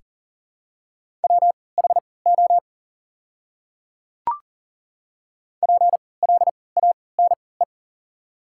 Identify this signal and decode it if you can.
Morse code, heard as a keyed tone, about 40 wpm, spaced as for 23 wpm, 700 Hz. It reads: WHO PLANE
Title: 3x Character Spacing - Top 500 Words - 40wpm